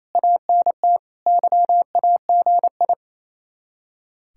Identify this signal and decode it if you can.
Morse code, heard as a keyed tone, about 36 wpm, 700 Hz. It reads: ANT YAGI